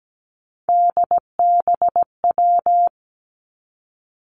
Morse code, heard as a keyed tone, about 17 wpm, 700 Hz. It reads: DBW
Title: Three Challenging Letter-Number Combinations 17wpm